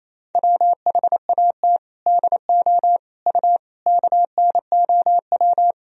Morse code, heard as a keyed tone, about 28 wpm, 700 Hz. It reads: WHAT DO U KNOW
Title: Sentences from Top 100 Words and Abbreviations 28wpm